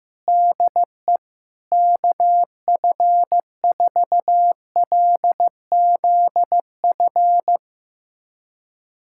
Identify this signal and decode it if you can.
Morse code, heard as a keyed tone, about 15 wpm, 700 Hz. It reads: DE KF4LZF
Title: USA Callsign DE Callsign 15wpm